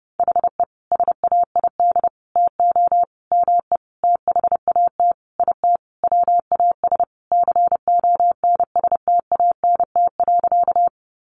Morse code, heard as a keyed tone, about 28 wpm, 700 Hz. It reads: HE SAID TO ME THAT IT WAS CONSTANT.
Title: Sentences from Top 1500 Words 30wpm